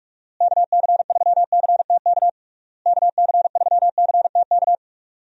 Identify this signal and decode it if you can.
Morse code, heard as a keyed tone, about 46 wpm, 700 Hz. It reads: KC3CTK KC3CTK
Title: USA Call Signs - 6 Characters - Repeated 3x - 45wpm